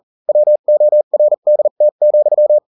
Morse code, heard as a keyed tone, about 40 wpm, spaced as for 35 wpm, 600 Hz. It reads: WORDT,